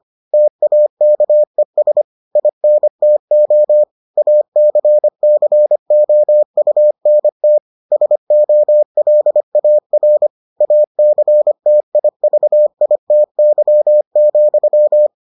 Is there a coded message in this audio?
Yes: Morse code from 0.3 to 15.1 s, heard as a keyed tone, about 25 wpm, 600 Hz. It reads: TAKES INTO ACCOUNT SOLAR ACTIVITY,